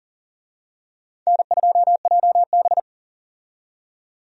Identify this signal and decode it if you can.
Morse code, heard as a keyed tone, about 40 wpm, 700 Hz. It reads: N2JB
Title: USA Call Signs - 4 Characters - 40wpm